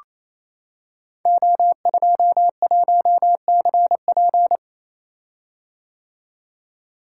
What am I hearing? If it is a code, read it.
Morse code, heard as a keyed tone, about 28 wpm, 700 Hz. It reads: O21CP
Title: Code Groups - 5 Letter:Numbers - 28wpm